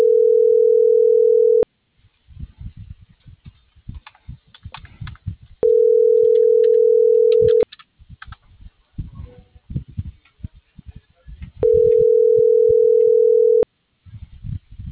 Ringback tone (North America), 440 + 480 Hz.